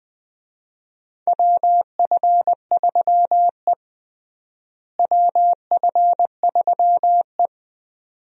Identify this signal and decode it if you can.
Morse code, heard as a keyed tone, about 21 wpm, 700 Hz. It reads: WF3E WF3E
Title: USA Call Signs - 4 Characters - Repeated 2x - 20wpm